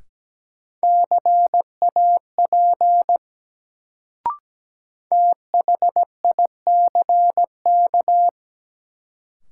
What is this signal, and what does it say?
Morse code, heard as a keyed tone, about 17 wpm, 700 Hz. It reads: CAP THICK
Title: Sentences from Top 1600 Words - New Word Drill Alt - 17wpm